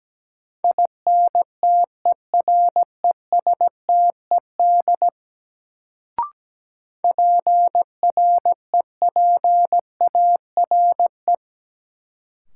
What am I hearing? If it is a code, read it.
Morse code, heard as a keyed tone, about 17 wpm, 700 Hz. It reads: INTERESTED PREPARE
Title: Sentences from Top 900 Words - New Word Drill - 17wpm